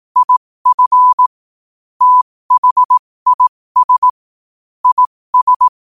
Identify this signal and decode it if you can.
Morse code, heard as a keyed tone, about 18 wpm, spaced as for 15 wpm, 1000 Hz. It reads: IF THIS IS
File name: qotdfiles / 15wpm